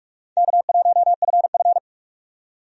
Morse code, heard as a keyed tone, about 45 wpm, 700 Hz. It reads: K1FF
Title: USA Call Signs - 4 Characters - 45wpm